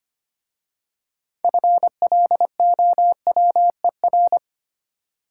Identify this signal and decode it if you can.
Morse code, heard as a keyed tone, about 25 wpm, 700 Hz. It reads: FLOWER
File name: practice / 25wpm / Easy Words with Hints - 25wpm